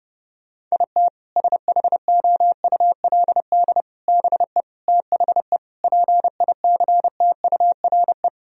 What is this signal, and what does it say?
Morse code, heard as a keyed tone, about 32 wpm, 700 Hz. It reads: IT SHOULD BE THE PICTURE
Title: Sentences from Top 1000 Words - Review Part 1 - 30wpm